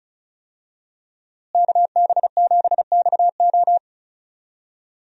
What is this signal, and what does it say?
Morse code, heard as a keyed tone, about 36 wpm, 700 Hz. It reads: KB7XO
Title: USA Call Signs - 5 Characters - Repeated 3x - 35wpm